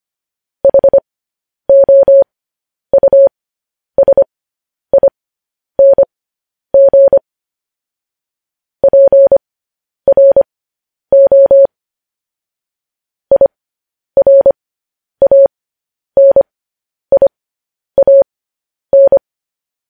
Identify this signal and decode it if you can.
Morse code, heard as a keyed tone, about 25 wpm, spaced as for 10 wpm, 550 Hz. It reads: HOUSING PRO IRANIAN